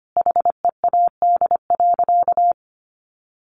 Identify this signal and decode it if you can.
Morse code, heard as a keyed tone, about 25 wpm, 700 Hz. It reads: HEAD.
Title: A Sound Of Thunder 25wpm